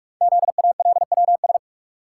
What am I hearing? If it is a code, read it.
Morse code, heard as a keyed tone, about 44 wpm, 700 Hz. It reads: ZALWS
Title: Code Groups - 5 Letters - 45wpm